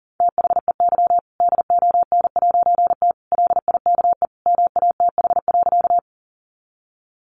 Morse code, heard as a keyed tone, about 40 wpm, 700 Hz. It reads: THEY DON'T LIKE MATH.
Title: Easy Phrases 40wpm